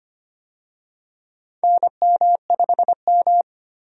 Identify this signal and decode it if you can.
Morse code, heard as a keyed tone, about 25 wpm, 700 Hz. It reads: NM5M
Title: USA Call Signs - 4 Characters - 25wpm